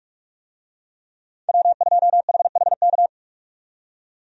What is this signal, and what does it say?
Morse code, heard as a keyed tone, about 46 wpm, 700 Hz. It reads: W2HHK
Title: USA Call Signs - 5 Characters - 45wpm